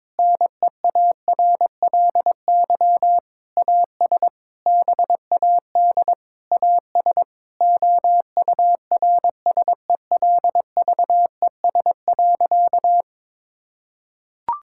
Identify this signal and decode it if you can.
Morse code, heard as a keyed tone, about 21 wpm, 700 Hz. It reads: NEARLY AS BAD AS OURSELVES.
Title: Sentences from Top 1600 Words 22wpm